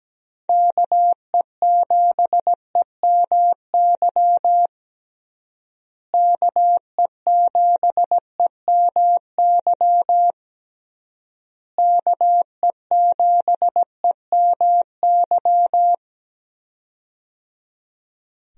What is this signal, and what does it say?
Morse code, heard as a keyed tone, about 17 wpm, 700 Hz. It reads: KE7EMY KE7EMY KE7EMY
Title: USA Call Signs - 6 Characters - Repeated 3x - 17wpm